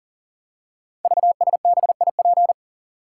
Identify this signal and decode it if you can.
Morse code, heard as a keyed tone, about 39 wpm, 700 Hz. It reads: VSBIP
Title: Code Groups - 5 Letter:Numbers - 40wpm